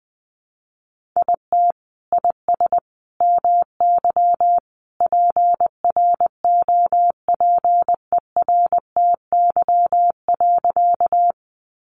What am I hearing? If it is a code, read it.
Morse code, heard as a keyed tone, about 19 wpm, 700 Hz. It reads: IT IS MY PROPERTY.
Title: Sentences from Top 1000 Words 20wpm